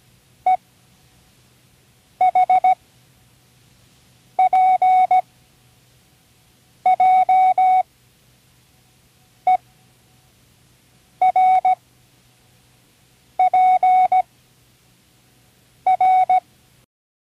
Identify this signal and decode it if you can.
Morse code, heard as a keyed tone, about 17 wpm, 740 Hz. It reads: E H P J E R P R
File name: Learning Morse Code (from Tune in the World)